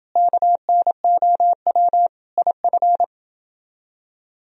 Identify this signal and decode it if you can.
Morse code, heard as a keyed tone, about 29 wpm, 700 Hz. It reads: KNOW IF